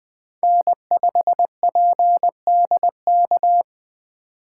Morse code, heard as a keyed tone, about 20 wpm, 700 Hz. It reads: N5PDK
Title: USA Call Signs - 5 Characters - 3x Character Spacing - 20wpm